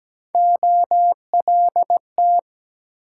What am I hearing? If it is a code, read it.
Morse code, heard as a keyed tone, about 23 wpm, 700 Hz. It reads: OLT